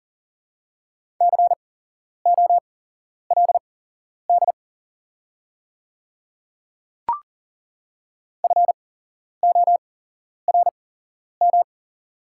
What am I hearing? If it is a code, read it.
Morse code, heard as a keyed tone, about 40 wpm, spaced as for 11 wpm, 700 Hz. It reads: COLD FORM
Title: 8x Character Spacing - Top 500 Words - 40wpm